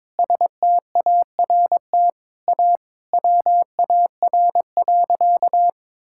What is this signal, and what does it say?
Morse code, heard as a keyed tone, about 22 wpm, 700 Hz. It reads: START A WAR.
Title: Sentences from Top 1000 Words - Review Part 6 - 22wpm